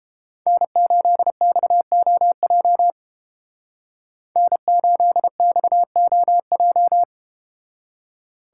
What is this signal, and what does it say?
Morse code, heard as a keyed tone, about 32 wpm, 700 Hz. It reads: N8XOJ N8XOJ